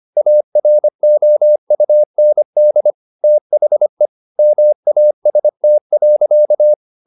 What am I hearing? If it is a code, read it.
Morse code, heard as a keyed tone, about 25 wpm, 600 Hz. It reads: AROUND THE MAST.